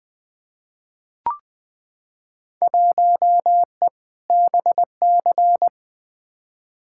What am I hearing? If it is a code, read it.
Morse code, heard as a keyed tone, about 20 wpm, 700 Hz. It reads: 1E BC
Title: Field Day Exchanges - 20wpm